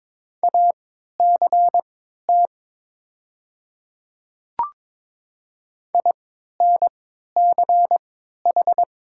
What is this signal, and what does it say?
Morse code, heard as a keyed tone, about 22 wpm, spaced as for 13 wpm, 700 Hz. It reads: ACT INCH
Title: 3x Character Spacing - Top 500 Words - 22wpm